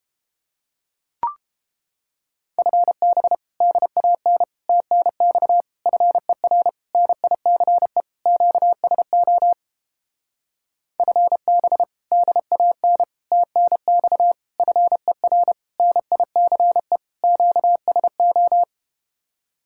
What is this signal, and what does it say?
Morse code, heard as a keyed tone, about 32 wpm, 700 Hz. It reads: FB DAN TNX FER NICE QSO FB DAN TNX FER NICE QSO